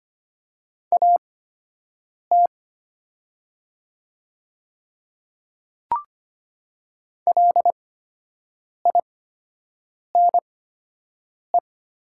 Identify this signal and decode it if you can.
Morse code, heard as a keyed tone, about 25 wpm, spaced as for 7 wpm, 700 Hz. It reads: AT LINE
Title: 8x Character Spacing - Top 200 Words - 25wpm